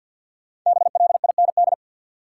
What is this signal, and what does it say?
Morse code, heard as a keyed tone, about 51 wpm, 700 Hz. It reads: BLIND
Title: Five Letter Words - 50wpm